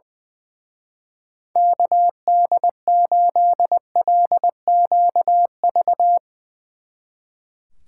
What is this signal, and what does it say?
Morse code, heard as a keyed tone, about 20 wpm, 700 Hz. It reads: KD8LQV
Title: Speed Racing - USA 2x3 Call Signs 20wpm